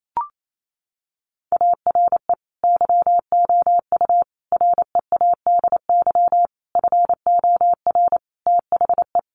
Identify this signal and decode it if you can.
Morse code, heard as a keyed tone, about 28 wpm, 700 Hz. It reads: ARE YOU READY FOR THE
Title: Sentences from Top 1300 Words 28wpm